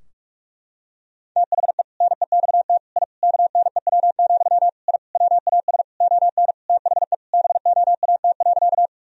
Morse code, heard as a keyed tone, about 46 wpm, 700 Hz. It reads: THE NEXT I KNEW, I WAS ON THE BOAT.
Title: Sentences from Top 1000 Words - Review Part 3 - 45wpm